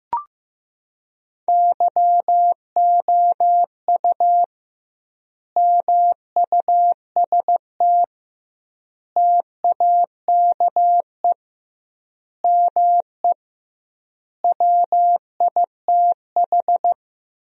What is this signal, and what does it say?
Morse code, heard as a keyed tone, about 15 wpm, 700 Hz. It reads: YOU MUST TAKE ME WITH
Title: 2x Word-Spacing - Sentences from Top 200 Words - 15wpm